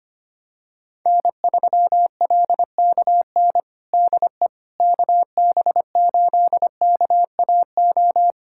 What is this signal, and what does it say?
Morse code, heard as a keyed tone, about 25 wpm, 700 Hz. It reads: N3LKN DE KB8KAO
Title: USA Callsign DE Callsign 25wpm